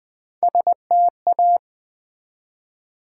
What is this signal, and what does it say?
Morse code, heard as a keyed tone, about 20 wpm, 700 Hz. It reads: STA
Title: Build A Word - 1 Letter At A Time - 3 to 5-Letter Words - 20wpm